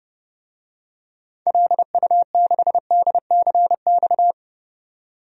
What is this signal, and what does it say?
Morse code, heard as a keyed tone, about 32 wpm, 700 Hz. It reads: LU6DCX
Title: DXCC-100 Call Signs - 6 Characters - 30wpm